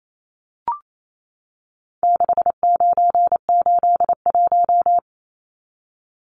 Morse code, heard as a keyed tone, about 28 wpm, 700 Hz. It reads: 6981
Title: Numbers - 4-Digit 28wpm